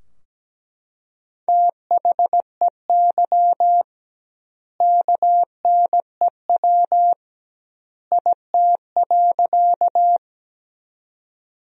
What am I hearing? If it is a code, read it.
Morse code, heard as a keyed tone, about 17 wpm, 700 Hz. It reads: THEY KNEW IT.
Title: 2x Word-Spacing - Sentences from Top 400 Words - 17wpm